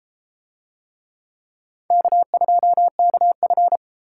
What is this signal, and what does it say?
Morse code, heard as a keyed tone, about 32 wpm, 700 Hz. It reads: K2KF